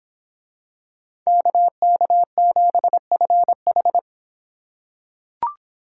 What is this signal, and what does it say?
Morse code, heard as a keyed tone, about 26 wpm, 700 Hz. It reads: KK7FH